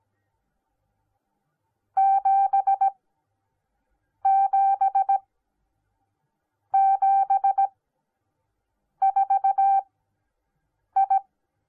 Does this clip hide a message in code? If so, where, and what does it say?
Yes: Morse code from 2.0 to 11.2 s, heard as a keyed tone, about 18 wpm, spaced as for 6 wpm, 780 Hz. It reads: 7774I